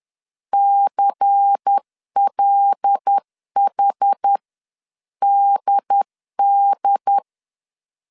Morse code, heard as a keyed tone, about 10 wpm, spaced as for 9 wpm, 800 Hz. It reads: CLHDD
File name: CW / DET / 1 6 1500C S C 10wpm 4252